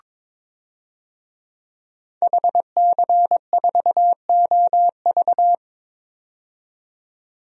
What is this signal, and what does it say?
Morse code, heard as a keyed tone, about 22 wpm, 700 Hz. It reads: HC4OV
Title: DXCC-100 Call Signs - 5 Characters - 22wpm